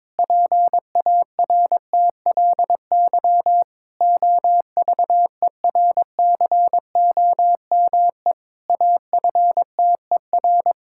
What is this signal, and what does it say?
Morse code, heard as a keyed tone, about 22 wpm, 700 Hz. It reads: PARTLY OVERCOME AFTER
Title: Pangram Challenge v2 - 22wpm